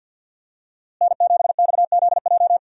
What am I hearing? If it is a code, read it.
Morse code, heard as a keyed tone, about 51 wpm, 700 Hz. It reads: N7XZJ
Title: USA Call Signs - 5 Characters - Repeated 3x - 50wpm